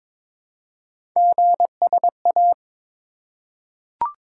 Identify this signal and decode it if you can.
Morse code, heard as a keyed tone, about 23 wpm, 700 Hz. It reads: GSA
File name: practice / 22wpm / Three Letter-Number Combinations - 22wpm